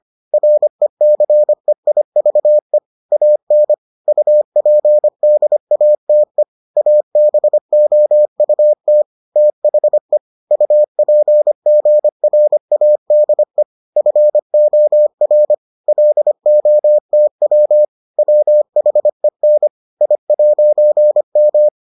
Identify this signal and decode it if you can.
Morse code, heard as a keyed tone, about 25 wpm, 600 Hz. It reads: RECEIVE AN UPDATE ABOUT THE UPGRADE FOR LOTW WHEN I'M